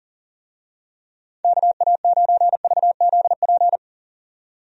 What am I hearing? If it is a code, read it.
Morse code, heard as a keyed tone, about 40 wpm, 700 Hz. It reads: KA9VZP
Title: USA Call Signs - 6 Characters - 40wpm